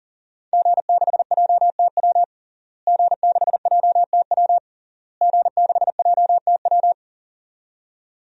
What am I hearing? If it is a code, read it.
Morse code, heard as a keyed tone, about 40 wpm, 700 Hz. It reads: G6JTW G6JTW G6JTW
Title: DXCC-100 Call Signs - 5 Characters - Repeated 3x - 40wpm